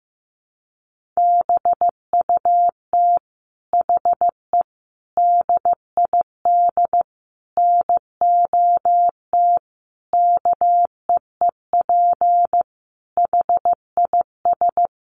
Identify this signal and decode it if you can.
Morse code, heard as a keyed tone, about 15 wpm, 700 Hz. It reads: BUT HE DID NOT KEEP HIS
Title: Sentences from Top 1500 Words 15wpm